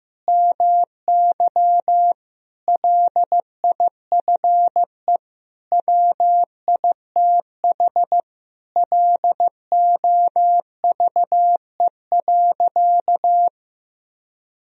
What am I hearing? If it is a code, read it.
Morse code, heard as a keyed tone, about 15 wpm, 700 Hz. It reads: MY LIFE WITH LOVE.
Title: Sentences from Top 1000 Words - Review Part 3 - 15wpm